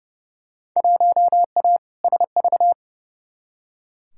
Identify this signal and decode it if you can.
Morse code, heard as a keyed tone, about 29 wpm, 700 Hz. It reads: 1A SV
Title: Field Day Exchanges Encore - 30wpm